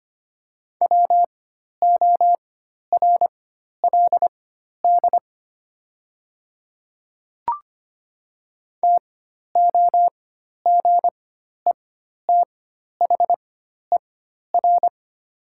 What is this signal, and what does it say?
Morse code, heard as a keyed tone, about 25 wpm, spaced as for 12 wpm, 700 Hz. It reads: WORLD TOGETHER
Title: 4x Character Spacing - Top 500 Words - 25wpm